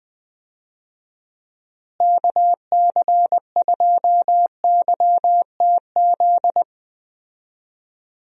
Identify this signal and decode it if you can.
Morse code, heard as a keyed tone, about 20 wpm, 700 Hz. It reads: KC2YTZ
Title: USA Call Signs - 6 Characters - Repeated 2x - 20wpm